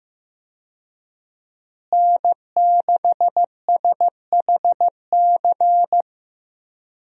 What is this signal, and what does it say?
Morse code, heard as a keyed tone, about 15 wpm, 700 Hz. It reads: N6SHC
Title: USA Call Signs - 5 Characters - Repeated 3x - 15wpm